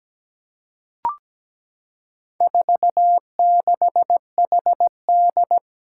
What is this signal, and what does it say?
Morse code, heard as a keyed tone, about 17 wpm, 700 Hz. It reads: 46HD
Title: Four Challenging Letter-Number Combinations 17wpm